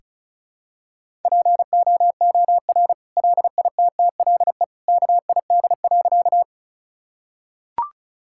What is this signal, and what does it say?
Morse code, heard as a keyed tone, about 34 wpm, 700 Hz. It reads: POOR LITTLE KID.
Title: Sentences from Top 1600 Words 35wpm